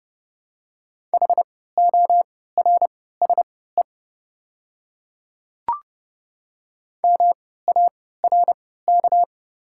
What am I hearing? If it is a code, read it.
Morse code, heard as a keyed tone, about 30 wpm, spaced as for 17 wpm, 690 Hz. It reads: HORSE MARK
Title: 3x Character Spacing - Top 300 Words - 30wpm